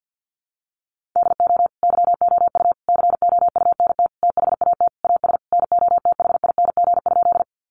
Morse code, heard as a keyed tone, about 50 wpm, 700 Hz. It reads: DO YOU COUNT THAT AS NOTHING?
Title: Sentences from Top 1500 Words 50wpm